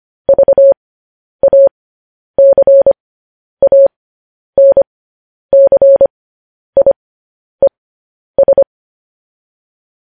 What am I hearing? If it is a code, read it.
Morse code, heard as a keyed tone, about 25 wpm, spaced as for 10 wpm, 550 Hz. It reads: VACANCIES
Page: news headlines in morse code at 10 wpm